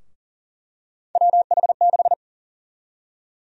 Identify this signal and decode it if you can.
Morse code, heard as a keyed tone, about 40 wpm, 700 Hz. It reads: WH6
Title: USA Call Sign Prefixes - 40wpm